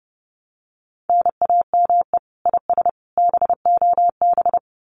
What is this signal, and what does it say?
Morse code, heard as a keyed tone, about 30 wpm, 700 Hz. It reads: NAME IS BOB
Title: Common QSO Phrases 30wpm